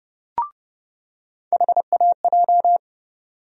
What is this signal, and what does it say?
Morse code, heard as a keyed tone, about 30 wpm, 700 Hz. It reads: HAJ
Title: USA Call Sign Postfixes - 3 Letters 30wpm